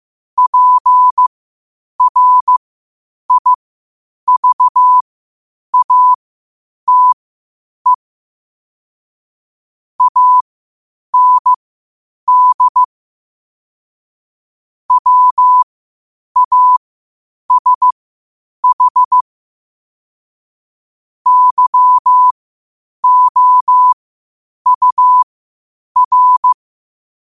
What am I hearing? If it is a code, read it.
Morse code, heard as a keyed tone, about 15 wpm, spaced as for 8 wpm, 1000 Hz. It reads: PRIVATE AND WASH YOUR
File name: qotdfiles / 8wpm